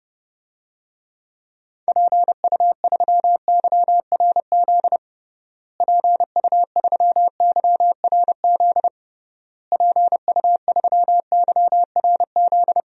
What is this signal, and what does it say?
Morse code, heard as a keyed tone, about 30 wpm, 700 Hz. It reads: PU3YRZ PU3YRZ PU3YRZ
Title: DXCC-100 Call Signs - 6 Characters - Repeated 3x - 30wpm